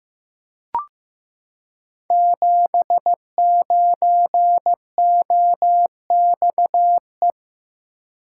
Morse code, heard as a keyed tone, about 15 wpm, 700 Hz. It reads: 79OXE
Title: Code Groups - 5 Letter:Numbers - 15wpm